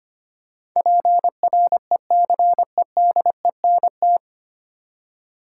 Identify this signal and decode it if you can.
Morse code, heard as a keyed tone, about 25 wpm, 700 Hz. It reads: PRECEDENT
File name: Nine Letter Words - 25wpm